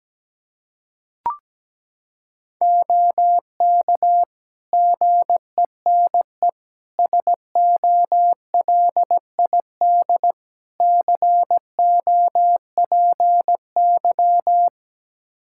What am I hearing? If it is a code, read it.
Morse code, heard as a keyed tone, about 17 wpm, 700 Hz. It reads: OK GENE SOLID COPY
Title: Practice QSO - Exchange 3 - More Info - 17wpm